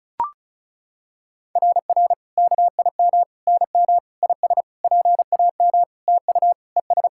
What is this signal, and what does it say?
Morse code, heard as a keyed tone, about 36 wpm, 690 Hz. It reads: RR KIM NM IS PAM TU ES
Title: SOTA QSO Practice 35wpm